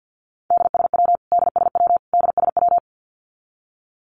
Morse code, heard as a keyed tone, about 50 wpm, 700 Hz. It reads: DSW DSW DSW
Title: Abbreviations Repeated 3x - 50wpm